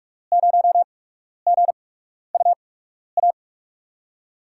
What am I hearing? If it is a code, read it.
Morse code, heard as a keyed tone, about 46 wpm, spaced as for 12 wpm, 700 Hz. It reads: 0GUA
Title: USA Call Signs - 6 Characters - 8x Character Spacing - 45wpm